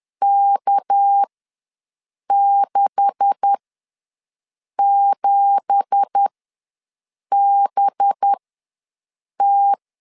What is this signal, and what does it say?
Morse code, heard as a keyed tone, about 10 wpm, spaced as for 6 wpm, 800 Hz. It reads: K67BT
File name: LEZIONI / 9 2867 ws 10wpm 4215